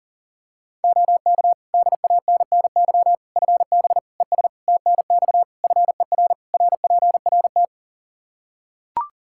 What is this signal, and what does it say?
Morse code, heard as a keyed tone, about 37 wpm, 700 Hz. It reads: OK DANNY FB ES TNX FER RPRT
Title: Practice QSO - Exchange 2 - Info - 40wpm